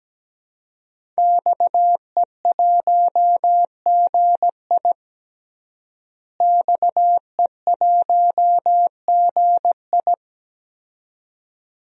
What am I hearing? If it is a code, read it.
Morse code, heard as a keyed tone, about 17 wpm, 700 Hz. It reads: XE1GI XE1GI
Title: DXCC-100 Call Signs - 5 Characters - Repeated 2x - 17wpm